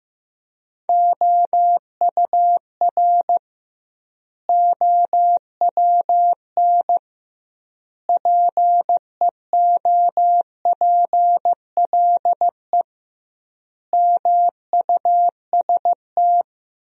Morse code, heard as a keyed tone, about 15 wpm, 700 Hz. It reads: OUR OWN PEOPLE MUST
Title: 2x Word-Spacing - Sentences from Top 300 Words - 15wpm